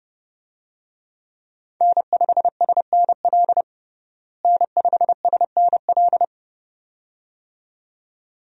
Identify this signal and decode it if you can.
Morse code, heard as a keyed tone, about 30 wpm, 700 Hz. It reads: N5SNL N5SNL
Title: USA Call Signs - 5 Characters - Repeated 2x - 30wpm